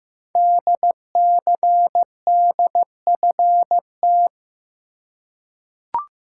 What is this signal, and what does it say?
Morse code, heard as a keyed tone, about 15 wpm, 700 Hz. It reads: DCDFT
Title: Code Groups - 5 Letters - 15wpm